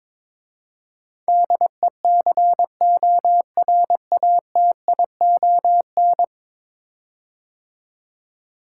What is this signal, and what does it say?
Morse code, heard as a keyed tone, about 22 wpm, 700 Hz. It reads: DECORATION
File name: practice / 22wpm / 10 Letter Words - First 3 Letters Given - 22wpm